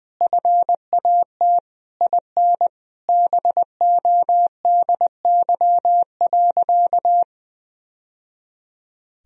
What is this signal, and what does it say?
Morse code, heard as a keyed tone, about 20 wpm, 700 Hz. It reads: FAT IN BODY.
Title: Sentences from Top 1000 Words - Review Part 4 - 20wpm